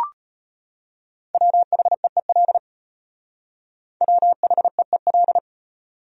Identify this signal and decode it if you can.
Morse code, heard as a keyed tone, about 31 wpm, 700 Hz. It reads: WHEEL WHEEL
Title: Speed-Racing Top 500 Words 25wpm